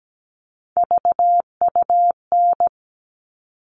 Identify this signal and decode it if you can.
Morse code, heard as a keyed tone, about 17 wpm, 700 Hz. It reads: VUN
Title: All Cut Numbers - 3-Digit 17wpm